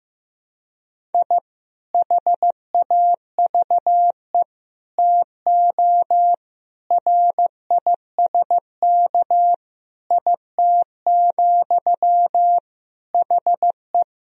Morse code, heard as a keyed tone, about 15 wpm, 700 Hz. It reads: I HAVE TO RISK IT, HE
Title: Sentences from Top 1000 Words - Review Part 4 - 15wpm